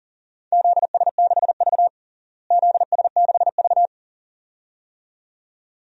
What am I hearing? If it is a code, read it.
Morse code, heard as a keyed tone, about 40 wpm, 700 Hz. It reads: ZS6V ZS6V
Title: DXCC-100 Call Signs - 4 Characters - Repeated 2x - 40wpm